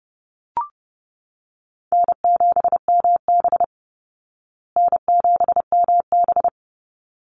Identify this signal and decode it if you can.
Morse code, heard as a keyed tone, about 30 wpm, 700 Hz. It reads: N7MB N7MB
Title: Easy Call Signs Repeated 2x - 30wpm